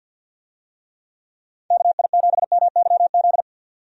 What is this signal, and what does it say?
Morse code, heard as a keyed tone, about 49 wpm, 700 Hz. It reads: KI7MYZ
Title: USA Call Signs - 6 Characters - 50wpm